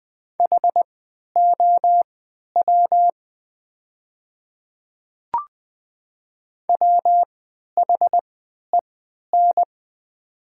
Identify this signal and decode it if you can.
Morse code, heard as a keyed tone, about 20 wpm, spaced as for 11 wpm, 690 Hz. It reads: HOW WHEN
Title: 3x Character Spacing - Top 100 Words - 20wpm